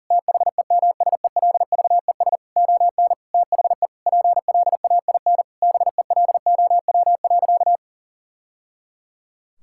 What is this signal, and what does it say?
Morse code, heard as a keyed tone, about 40 wpm, 700 Hz. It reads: THEMSELVES ON THE PLAIN BELOW.
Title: Sentences from Top 1000 Words - Review Part 5 - 40wpm